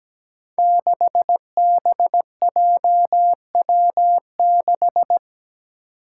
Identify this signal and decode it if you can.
Morse code, heard as a keyed tone, about 17 wpm, 700 Hz. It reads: 6BJW6
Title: Five Challenging Letter-Number Combinations 17wpm